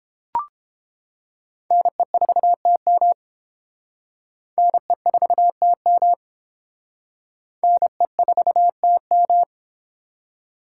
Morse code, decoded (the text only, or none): NE4TM NE4TM NE4TM